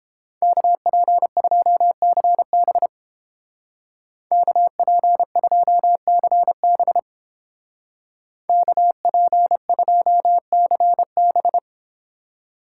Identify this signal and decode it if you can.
Morse code, heard as a keyed tone, about 31 wpm, 700 Hz. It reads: KP2CB KP2CB KP2CB